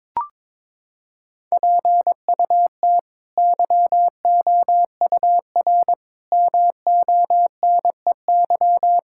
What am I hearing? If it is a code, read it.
Morse code, heard as a keyed tone, about 22 wpm, 700 Hz. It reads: PUT YOUR MONEY
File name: practice / 22wpm / Sets of 7 Words - 22wpm